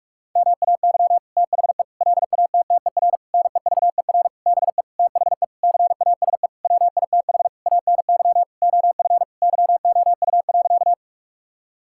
Morse code, heard as a keyed tone, about 46 wpm, 700 Hz. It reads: MAY THE LATTER NEVER BE THE CASE WITH ANY OF YOU.
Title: Sentences from Top 1600 Words 45wpm